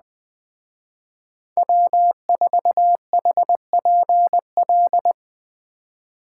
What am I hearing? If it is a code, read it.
Morse code, heard as a keyed tone, about 21 wpm, 700 Hz. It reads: W4HPL